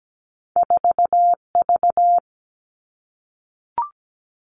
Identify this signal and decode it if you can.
Morse code, heard as a keyed tone, about 17 wpm, 700 Hz. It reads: 4V
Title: Two Challenging Letter-Number Combinations 17wpm